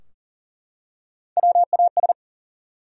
Morse code, heard as a keyed tone, about 40 wpm, 700 Hz. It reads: WAS
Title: Top 100 Words 40wpm